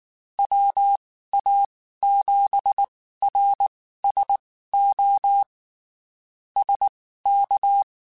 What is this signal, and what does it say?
Morse code, heard as a keyed tone, about 19 wpm, spaced as for 14 wpm, 800 Hz. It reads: WA7RSO SK
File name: (spaced) / QSOs Spaced-mp3s / kc7sac @18 sp